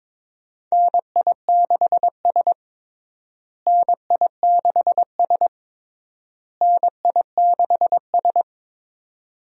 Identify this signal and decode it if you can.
Morse code, heard as a keyed tone, about 22 wpm, 700 Hz. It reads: NI6S NI6S NI6S
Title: USA Call Signs - 4 Characters - Repeated 3x - 22wpm